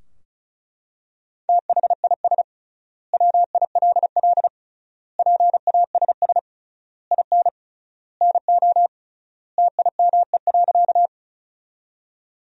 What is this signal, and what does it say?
Morse code, heard as a keyed tone, about 36 wpm, 700 Hz. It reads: THIS WILL PASS IN NO TIME.
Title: 3x Word-Spacing - Sentences from Top 400 Words - 35wpm